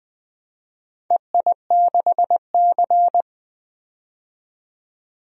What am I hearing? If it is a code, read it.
Morse code, heard as a keyed tone, about 20 wpm, 700 Hz. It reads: EI6C
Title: DXCC-100 Call Signs - 4 Characters - 20wpm